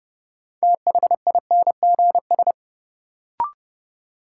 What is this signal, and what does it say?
Morse code, heard as a keyed tone, about 30 wpm, 700 Hz. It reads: THINGS
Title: Sentences from Top 1400 Words - New Word Drill Alt - 30wpm